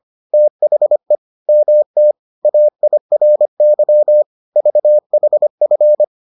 Morse code, heard as a keyed tone, about 25 wpm, 600 Hz. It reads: THE MT AIRY VHF